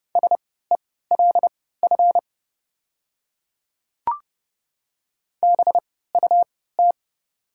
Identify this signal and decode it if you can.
Morse code, heard as a keyed tone, about 32 wpm, spaced as for 17 wpm, 700 Hz. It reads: SELF BUT
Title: 3x Character Spacing - Top 300 Words - 30wpm